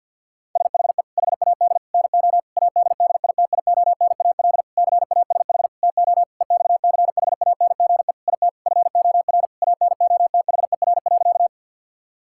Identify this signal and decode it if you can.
Morse code, heard as a keyed tone, about 50 wpm, 700 Hz. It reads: SHE HAD NO ADDITIONAL CASH TO EXCHANGE IT FOR ANOTHER.
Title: Sentences from Top 1000 Words - Review Part 6 - 50wpm